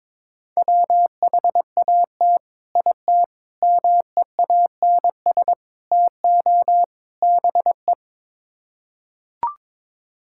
Morse code, heard as a keyed tone, about 21 wpm, 700 Hz. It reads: WHAT IT MEANS TO BE
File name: practice / 22wpm / Build A Phrase - 5-Word Phrases - 22wpm